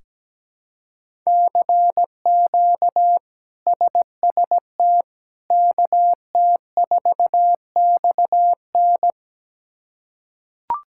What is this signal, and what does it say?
Morse code, heard as a keyed tone, about 17 wpm, 700 Hz. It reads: CQ SST KT4XN
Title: K1USN SST QSO Practice - 17wpm